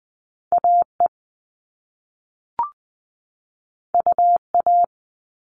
Morse code, heard as a keyed tone, about 20 wpm, 700 Hz. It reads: AE UA
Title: All Cut Numbers - 2-Digit 20wpm